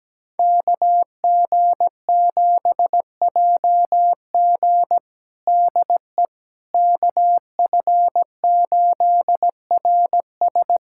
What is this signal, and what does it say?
Morse code, heard as a keyed tone, about 17 wpm, 700 Hz. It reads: KG7JG DE KF8RS
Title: USA Callsign DE Callsign - 5 Characters - 17wpm